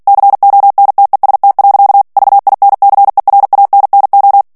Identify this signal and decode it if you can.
Morse code, heard as a keyed tone, about 48 wpm, 800 Hz. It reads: CONTEST. VINCERANNO
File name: ITALIEN texte clair 2 vitesse 45 mots minute